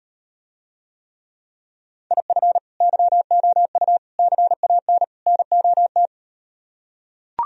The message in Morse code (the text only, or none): IF YOU CAN NOT